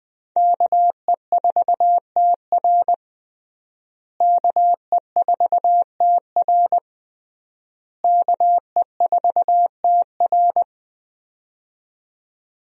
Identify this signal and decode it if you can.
Morse code, heard as a keyed tone, about 20 wpm, 700 Hz. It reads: KE4TR KE4TR KE4TR
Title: USA Call Signs - 5 Characters - Repeated 3x - 20wpm